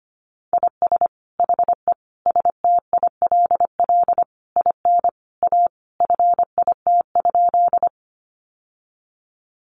Morse code, heard as a keyed tone, about 25 wpm, 700 Hz. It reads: IS HE STILL IN A FIT?
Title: Sentences from Top 800 Words 25wpm